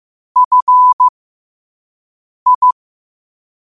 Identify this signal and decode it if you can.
Morse code, heard as a keyed tone, about 15 wpm, spaced as for 5 wpm, 1000 Hz. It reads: FI
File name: qotdfiles / 5wpm